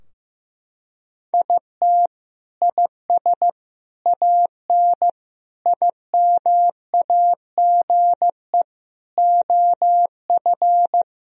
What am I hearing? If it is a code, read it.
Morse code, heard as a keyed tone, about 16 wpm, 700 Hz. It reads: IT IS AN IMAGE OF
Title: Sentences from Top 1000 Words - Review Part 4 - 15wpm